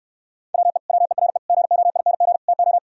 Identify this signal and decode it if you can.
Morse code, heard as a keyed tone, about 66 wpm, 700 Hz. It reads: WE WERE GOING TO